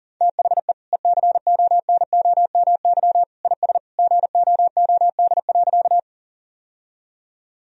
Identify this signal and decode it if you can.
Morse code, heard as a keyed tone, about 40 wpm, 700 Hz. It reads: THE ECONOMY IS GOOD.
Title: Sentences from Top 1000 Words - Review Part 3 - 40wpm